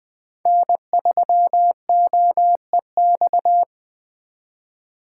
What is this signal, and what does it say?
Morse code, heard as a keyed tone, about 20 wpm, 700 Hz. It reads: N3OEX